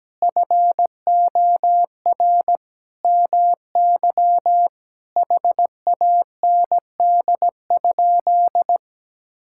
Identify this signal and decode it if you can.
Morse code, heard as a keyed tone, about 17 wpm, 700 Hz. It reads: FOR MY HAND?
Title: Sentences from Top 1600 Words 17wpm